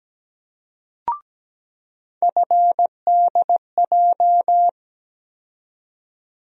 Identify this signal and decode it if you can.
Morse code, heard as a keyed tone, about 17 wpm, 700 Hz. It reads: FDJ
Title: USA Call Sign Postfixes - 3 Letters 17wpm